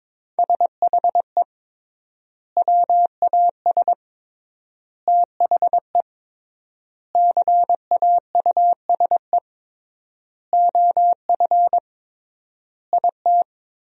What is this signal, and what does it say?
Morse code, heard as a keyed tone, about 22 wpm, 700 Hz. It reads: SHE WAS THE CAUSE OF IT
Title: 3x Word-Spacing - Sentences from Top 200 Words - 22wpm